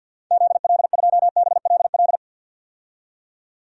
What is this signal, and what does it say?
Morse code, heard as a keyed tone, about 48 wpm, 700 Hz. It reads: ZL2BLL
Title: DXCC-100 Call Signs - 6 Characters - 50wpm